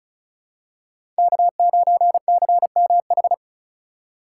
Morse code, heard as a keyed tone, about 36 wpm, 700 Hz. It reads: K9CMH
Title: USA Call Signs - 5 Characters - 3x Character Spacing - 35wpm